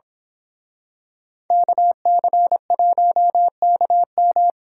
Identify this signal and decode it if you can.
Morse code, heard as a keyed tone, about 26 wpm, 700 Hz. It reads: KC1KM
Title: Speed Racing - USA 2x2 Call Signs 22wpm